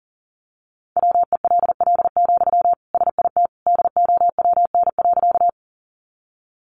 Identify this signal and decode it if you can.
Morse code, heard as a keyed tone, about 40 wpm, 700 Hz. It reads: WELL, SIT DOWN.
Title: Sentences from Top 500 Words 40wpm